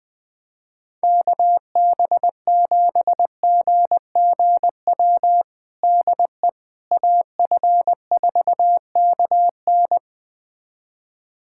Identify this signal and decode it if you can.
Morse code, heard as a keyed tone, about 20 wpm, 700 Hz. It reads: KB7GGW DE AF4KN
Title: USA Callsign DE Callsign 20wpm